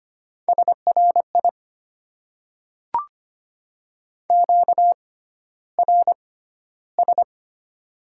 Morse code, heard as a keyed tone, about 25 wpm, spaced as for 13 wpm, 690 Hz. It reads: SRI QRS
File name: Top 90 QSO Elements - 6x Character Spacing - 25wpm